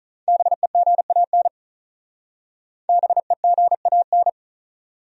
Morse code, heard as a keyed tone, about 45 wpm, 700 Hz. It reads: BEGAN BEGAN